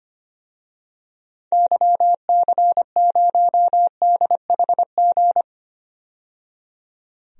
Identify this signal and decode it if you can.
Morse code, heard as a keyed tone, about 25 wpm, 700 Hz. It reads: YC0DHG
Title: DXCC-100 Call Signs - 6 Characters - 25wpm